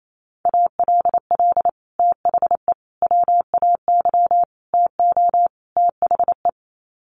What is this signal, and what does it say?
Morse code, heard as a keyed tone, about 28 wpm, 700 Hz. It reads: ALL THE WAY TO THE
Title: Sets of 5 Words 28wpm